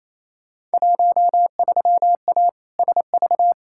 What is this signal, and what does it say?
Morse code, heard as a keyed tone, about 28 wpm, 700 Hz. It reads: 13A SV
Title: Field Day Exchanges Encore - 28wpm